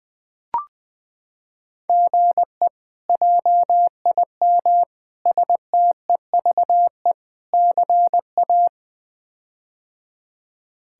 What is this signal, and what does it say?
Morse code, heard as a keyed tone, about 20 wpm, 700 Hz. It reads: GE JIM STEVE CA
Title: K1USN SST QSO Practice - 20wpm